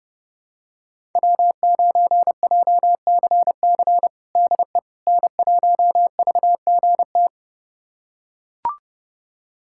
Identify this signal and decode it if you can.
Morse code, heard as a keyed tone, about 32 wpm, 700 Hz. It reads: W9JCC DE N1VGT
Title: USA Callsign DE Callsign - 5 Characters - 30wpm